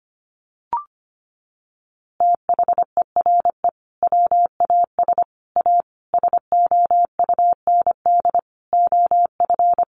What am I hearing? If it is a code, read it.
Morse code, heard as a keyed tone, about 25 wpm, 700 Hz. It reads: THERE WAS A SOUND OF
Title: A Sound Of Thunder 25wpm